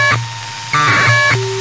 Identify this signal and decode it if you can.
Packet radio (AX.25, as in APRS).
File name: Sohla-1 1200bd packet HiMode by ja0caw